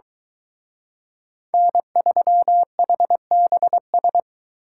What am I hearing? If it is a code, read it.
Morse code, heard as a keyed tone, about 23 wpm, 700 Hz. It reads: N3HBS